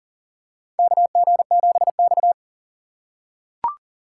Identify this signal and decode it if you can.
Morse code, heard as a keyed tone, about 40 wpm, 700 Hz. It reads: KG7X
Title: USA Call Signs - 4 Characters - 4x Character Spacing - 40wpm